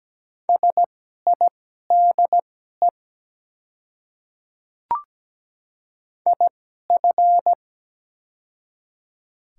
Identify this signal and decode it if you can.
Morse code, heard as a keyed tone, about 17 wpm, spaced as for 12 wpm, 700 Hz. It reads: SIDE IF
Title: 2x Character Spacing - Top 100 Words - 17wpm